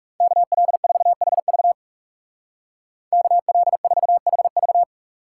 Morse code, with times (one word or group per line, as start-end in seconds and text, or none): 0.2-1.7 KL4HV
3.1-4.8 KL4HV